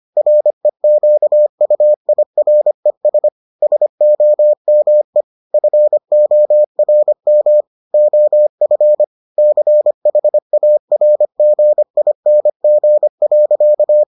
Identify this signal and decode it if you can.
Morse code, heard as a keyed tone, about 25 wpm, 600 Hz. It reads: REQUIRES SOME FORM OF CHARGING.